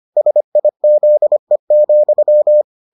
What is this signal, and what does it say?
Morse code, heard as a keyed tone, about 25 wpm, 600 Hz. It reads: SIZE,